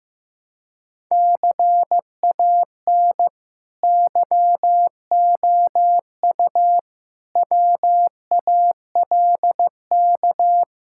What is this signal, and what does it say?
Morse code, heard as a keyed tone, about 15 wpm, 700 Hz. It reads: CAN YOU WALK
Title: Sentences from Top 1000 Words - Review Part 6 - 15wpm